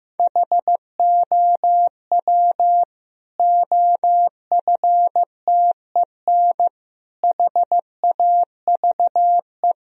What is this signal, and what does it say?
Morse code, heard as a keyed tone, about 15 wpm, 700 Hz. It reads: HOW OFTEN HAVE
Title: Sentences from Top 1000 Words - Review Part 1 - 15wpm